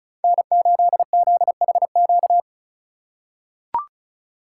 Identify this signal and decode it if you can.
Morse code, heard as a keyed tone, about 36 wpm, 700 Hz. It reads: N8ZHQ
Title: USA Call Signs - 5 Characters - 4x Character Spacing - 35wpm